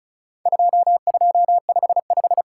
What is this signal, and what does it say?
Morse code, heard as a keyed tone, about 36 wpm, 700 Hz. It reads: 2255
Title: Worked All Europe DX Contest - QTC Exchange - 35wpm